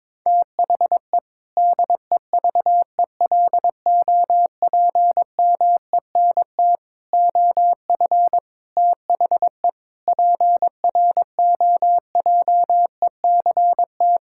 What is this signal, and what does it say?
Morse code, heard as a keyed tone, about 21 wpm, 700 Hz. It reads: THE DEVELOPMENT OF THE PROJECT
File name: practice / 22wpm / Sentences with 3-Word Phrases - 22wpm